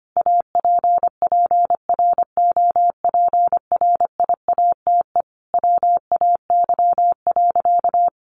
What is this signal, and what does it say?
Morse code, heard as a keyed tone, about 25 wpm, 700 Hz. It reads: APPROPRIATE WAY.
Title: Sentences from Top 900 Words 25wpm